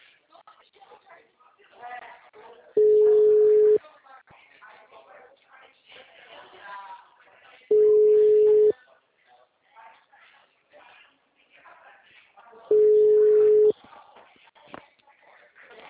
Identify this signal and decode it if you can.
Ringback tone (Europe, 425 Hz), with speech.
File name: out- -2009- - - .75667